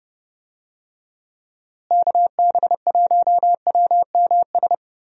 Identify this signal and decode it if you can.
Morse code, heard as a keyed tone, about 28 wpm, 700 Hz. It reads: KB1WMS
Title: USA Call Signs - 6 Characters - Repeated 2x - 30wpm